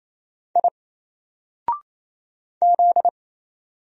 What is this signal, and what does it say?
Morse code, heard as a keyed tone, about 27 wpm, 700 Hz. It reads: I Z
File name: Single-Letter-Number-28wpm